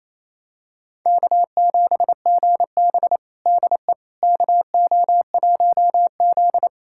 Morse code, heard as a keyed tone, about 28 wpm, 700 Hz. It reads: K7GB DE KO1Z
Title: USA Callsign DE Callsign - 4 Characters - 28wpm